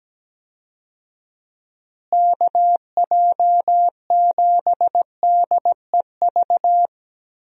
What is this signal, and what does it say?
Morse code, heard as a keyed tone, about 17 wpm, 700 Hz. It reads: KJ7DEV
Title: USA Call Signs - 6 Characters - Repeated 2x - 17wpm